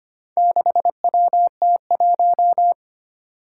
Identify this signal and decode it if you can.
Morse code, heard as a keyed tone, about 25 wpm, 700 Hz. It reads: 6WT1
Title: Four Challenging Letter-Number Combinations 25wpm